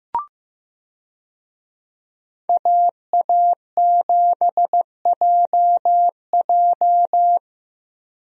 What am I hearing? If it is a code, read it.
Morse code, heard as a keyed tone, about 15 wpm, 700 Hz. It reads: AA7JJ
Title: USA Call Signs - 5 Characters - Repeated 3x - 15wpm